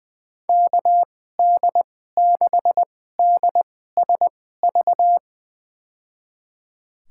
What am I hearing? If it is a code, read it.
Morse code, heard as a keyed tone, about 20 wpm, spaced as for 14 wpm, 700 Hz. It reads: KD6DSV
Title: USA Call Signs - 6 Characters - 2x Character Spacing - 20wpm